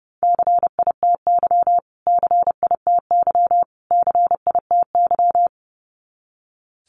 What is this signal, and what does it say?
Morse code, heard as a keyed tone, about 30 wpm, 700 Hz. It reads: CITY CITY CITY
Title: Top 500 Words Repeated 30wpm